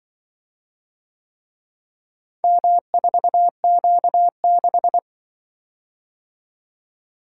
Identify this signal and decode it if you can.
Morse code, heard as a keyed tone, about 25 wpm, 700 Hz. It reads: M4Q6
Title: Speed-Racing - Four Letter-Number Combinations - 20wpm